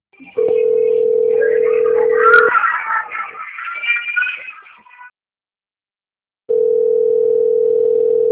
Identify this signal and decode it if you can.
Ringback tone (North America), 440 + 480 Hz.
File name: out- - - - - .26974